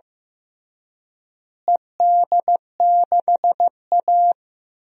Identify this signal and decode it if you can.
Morse code, heard as a keyed tone, about 15 wpm, 700 Hz. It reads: ED6A
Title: DXCC-100 Call Signs - 4 Characters - Repeated 3x - 15wpm